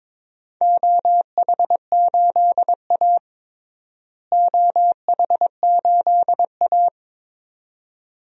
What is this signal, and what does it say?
Morse code, heard as a keyed tone, about 22 wpm, 700 Hz. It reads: OH8A OH8A
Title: DXCC-100 Call Signs - 4 Characters - Repeated 2x - 22wpm